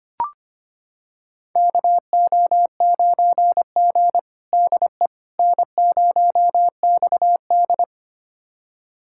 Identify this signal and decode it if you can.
Morse code, heard as a keyed tone, about 25 wpm, 700 Hz. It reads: KO9G DE N0XD
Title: USA Callsign DE Callsign - 4 Characters - 25wpm